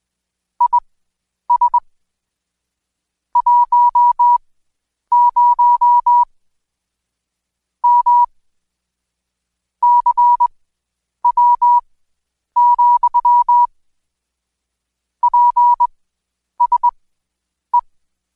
Morse code, heard as a keyed tone, about 20 wpm, spaced as for 9 wpm, 970 Hz. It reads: IS 10 M CW, PSE
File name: Code Quick / AudioDownld2 / Seg115-Day20-DayDrill 20